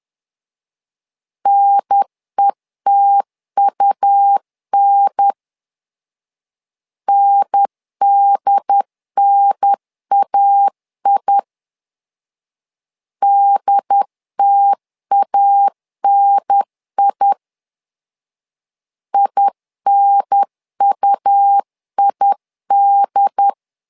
Morse code, heard as a keyed tone, about 10 wpm, 800 Hz. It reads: NETUN NDNAI DTANI INUID
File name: LEZIONI / 2 UHDN / 2 UHDN 10wpm 1904